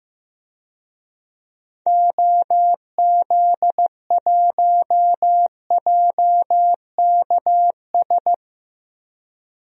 Morse code, heard as a keyed tone, about 15 wpm, 700 Hz. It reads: OZ1JKS
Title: DXCC-100 Call Signs - 6 Characters - Repeated 2x - 15wpm